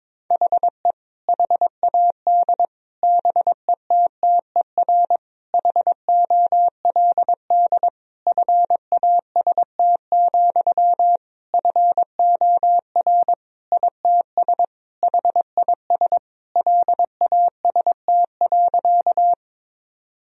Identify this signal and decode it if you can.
Morse code, heard as a keyed tone, about 22 wpm, 700 Hz. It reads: HE HAD BETTER HOLD FAST, FOR ITS HIS LAST.
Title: Sentences from Top 1000 Words - Review Part 2 - 22wpm